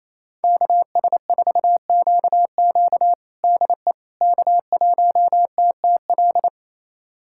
Morse code, heard as a keyed tone, about 28 wpm, 690 Hz. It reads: KS4QQ DE K1TTL
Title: USA Callsign DE Callsign - 5 Characters - 28wpm